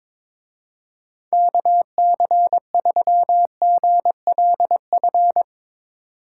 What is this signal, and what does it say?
Morse code, heard as a keyed tone, about 24 wpm, 700 Hz. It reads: KC3GLF
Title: Speed Racing - USA Call Signs 22wpm